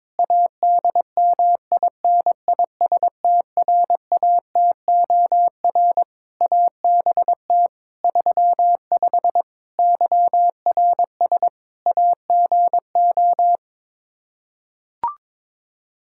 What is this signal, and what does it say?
Morse code, heard as a keyed tone, about 22 wpm, 700 Hz. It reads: ADMINISTRATOR ABT 35 YRS AGO
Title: Practice QSO - Exchange 3 - More Info - 22wpm